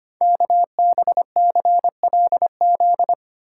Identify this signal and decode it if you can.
Morse code, heard as a keyed tone, about 25 wpm, 700 Hz. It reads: KBCLZ
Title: Code Groups - 5 Letters - 25wpm